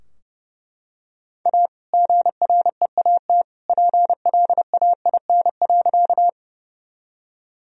Morse code, heard as a keyed tone, about 30 wpm, 700 Hz. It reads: A GREAT PLAIN.
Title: Sentences from Top 1000 Words - Review Part 2 - 30wpm